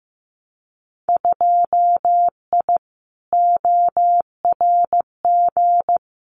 Morse code, heard as a keyed tone, about 15 wpm, 700 Hz. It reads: 2I ORG
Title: Winter Field Day Exchanges Encore 15wpm